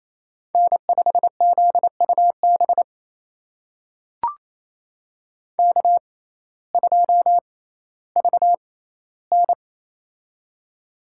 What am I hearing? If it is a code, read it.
Morse code, heard as a keyed tone, about 28 wpm, spaced as for 15 wpm, 700 Hz. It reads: N5ZUB K2VN
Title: USA Call Signs - 6x Character Spacing - 28wpm